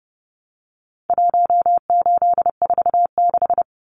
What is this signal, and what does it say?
Morse code, heard as a keyed tone, about 29 wpm, 700 Hz. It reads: 1846
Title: Numbers - Contest Style - 4-Digit 30wpm